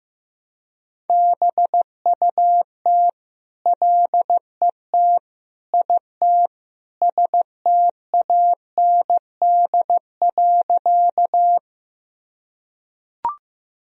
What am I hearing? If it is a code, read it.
Morse code, heard as a keyed tone, about 16 wpm, 700 Hz. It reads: BUT LET IT STAND.
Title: Sentences from Top 1000 Words - Review Part 1 - 15wpm